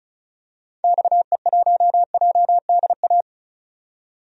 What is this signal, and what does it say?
Morse code, heard as a keyed tone, about 36 wpm, 700 Hz. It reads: XE1JDA
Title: DXCC-100 Call Signs - 6 Characters - 35wpm